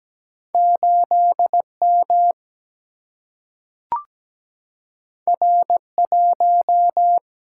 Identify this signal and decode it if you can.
Morse code, heard as a keyed tone, about 17 wpm, 700 Hz. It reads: 8M R1
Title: Single-Letter Single-Number 17wpm